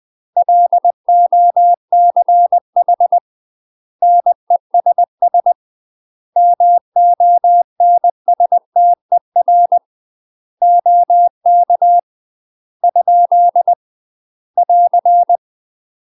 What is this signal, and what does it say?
Morse code, heard as a keyed tone, about 20 wpm, 700 Hz. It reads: LOCH NESS MONSTER OK ? <AR>
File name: UK MASTER QSOs TRAINING / 20WPM / UK MASTER QSOs 0012